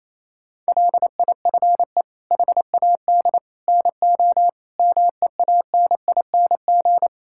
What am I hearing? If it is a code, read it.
Morse code, heard as a keyed tone, about 28 wpm, 700 Hz. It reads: LIFE HAD NO MEANING
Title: Sentences from Top 1000 Words - Review Part 6 - 28wpm